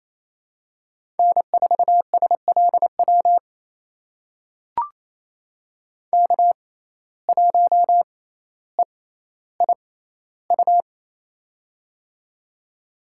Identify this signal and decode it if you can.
Morse code, heard as a keyed tone, about 27 wpm, spaced as for 14 wpm, 700 Hz. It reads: N4SLW K1EIU
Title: USA Call Signs - 5 Characters - 6x Character Spacing - 28wpm